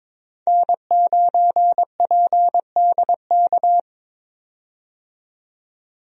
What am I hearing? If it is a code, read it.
Morse code, heard as a keyed tone, about 22 wpm, 700 Hz. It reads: N9PDK
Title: USA Call Signs - 5 Characters - 22wpm